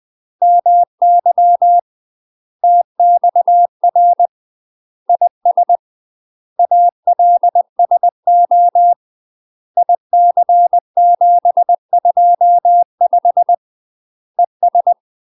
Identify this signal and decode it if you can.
Morse code, heard as a keyed tone, about 20 wpm, 700 Hz. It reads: MY TXR IS ALSO IC725 ES
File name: UK MASTER QSOs TRAINING / 20WPM / UK MASTER QSOs 0007